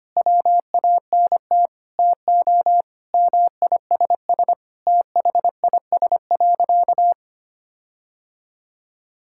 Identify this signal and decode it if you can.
Morse code, heard as a keyed tone, about 25 wpm, 700 Hz. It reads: WANT TO MISS THIS.
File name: practice / 25wpm / Sentences with 4-Word Phrases - 25wpm